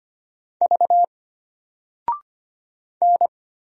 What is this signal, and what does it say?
Morse code, heard as a keyed tone, about 25 wpm, 700 Hz. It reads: V N
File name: Single-Letter-Number-25wpm